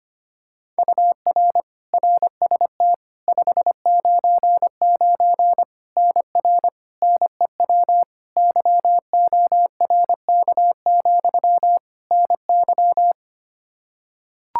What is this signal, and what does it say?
Morse code, heard as a keyed tone, about 25 wpm, 700 Hz. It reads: UR RST 599 NR NEW YORK, NY
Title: K1USN SST QSO Practice 25wpm